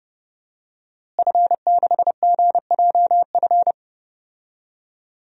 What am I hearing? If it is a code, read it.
Morse code, heard as a keyed tone, about 32 wpm, 700 Hz. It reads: F6GJF
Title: DXCC-100 Call Signs - 5 Characters - Repeated 3x - 30wpm